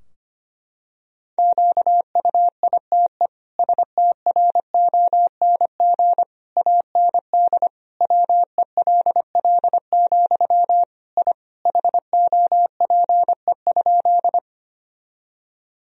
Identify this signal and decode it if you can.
Morse code, heard as a keyed tone, about 25 wpm, 700 Hz. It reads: QUITE STRONG AND WELL, I HOPE?
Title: Sentences from Top 1000 Words - Review Part 3 - 25wpm